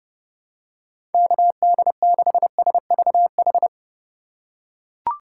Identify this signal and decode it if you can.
Morse code, heard as a keyed tone, about 31 wpm, 700 Hz. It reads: KD6SVH
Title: USA Call Signs - 6 Characters - 3x Character Spacing - 30wpm